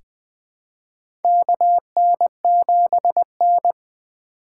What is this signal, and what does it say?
Morse code, heard as a keyed tone, about 20 wpm, 700 Hz. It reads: KN7N
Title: USA Call Signs - 4 Characters - 2x Character Spacing - 20wpm